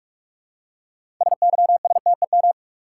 Morse code, heard as a keyed tone, about 46 wpm, 700 Hz. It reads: SYSTEM
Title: Six Letter Words - 45wpm